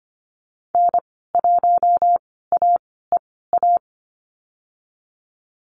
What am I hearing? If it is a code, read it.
Morse code, heard as a keyed tone, about 25 wpm, spaced as for 16 wpm, 700 Hz. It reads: N1AEA
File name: Lesson 15g - Number 5 - Call Signs 25wpm